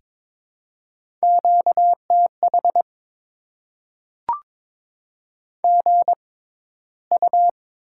Morse code, heard as a keyed tone, about 22 wpm, 690 Hz. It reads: Q T H G U
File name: Top 20 QSO Elements - 6x Character Spacing - 22wpm